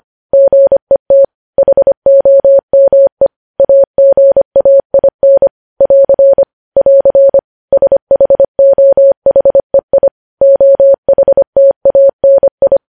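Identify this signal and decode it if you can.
Morse code, heard as a keyed tone, about 25 wpm, 550 Hz. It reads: GET HOME AGAIN <AR> <AR> SHOHEI OHTANI
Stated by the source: Morse news headlines sent at 25 wpm.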